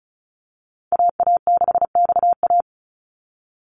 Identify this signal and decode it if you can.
Morse code, heard as a keyed tone, about 36 wpm, 700 Hz. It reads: AA6XA
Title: NA SOTA Activator Callsigns 35wpm